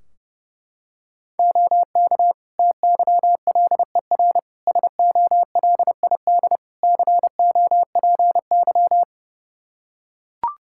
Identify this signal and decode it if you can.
Morse code, heard as a keyed tone, about 32 wpm, 700 Hz. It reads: OK TYLER SOLID COPY
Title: Practice QSO - Exchange 3 - More Info - 30wpm